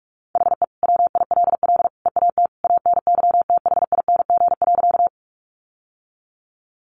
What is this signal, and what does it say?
Morse code, heard as a keyed tone, about 46 wpm, 700 Hz. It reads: HE WILL EAT ANYTHING.
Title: Sentences from Top 1000 Words 45wpm